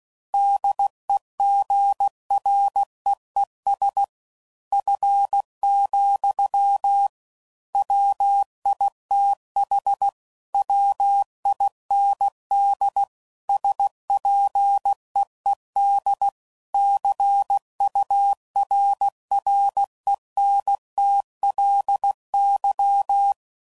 Morse code, heard as a keyed tone, about 16 wpm, 800 Hz. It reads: DEGREES F, WITH WIND SPEED CURRENTLY
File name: CW Code Groups @15WPM / Weather Sets / Weather 5 12